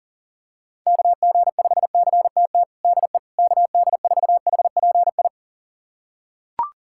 Morse code, heard as a keyed tone, about 37 wpm, 700 Hz. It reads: KG5CTT DE KD4HPI
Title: USA Callsign DE Callsign 40wpm